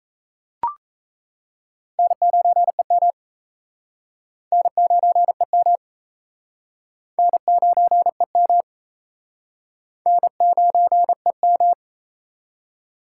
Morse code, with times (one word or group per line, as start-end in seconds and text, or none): 2.0-3.1 N9EM
4.5-5.8 N9EM
7.2-8.6 N9EM
10.1-11.7 N9EM